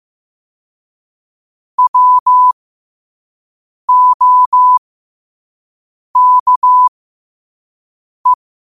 Morse code, heard as a keyed tone, about 15 wpm, spaced as for 5 wpm, 1000 Hz. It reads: WOKE
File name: qotdfiles / 5wpm